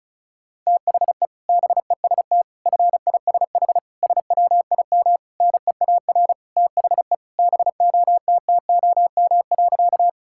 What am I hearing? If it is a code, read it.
Morse code, heard as a keyed tone, about 34 wpm, 690 Hz. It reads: THE BEST FISH SWIM NEAR THE BOTTOM.
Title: Sentences from Top 1000 Words - Review Part 6 - 35wpm